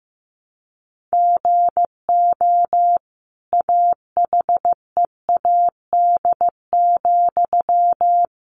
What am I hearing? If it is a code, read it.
Morse code, heard as a keyed tone, about 15 wpm, 700 Hz. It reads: GO AHEAD,
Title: Sentences from Top 1100 Words 15wpm